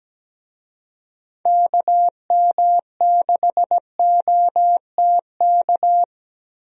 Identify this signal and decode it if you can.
Morse code, heard as a keyed tone, about 17 wpm, 700 Hz. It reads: KM6OTK
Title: USA Call Signs - 6 Characters - 17wpm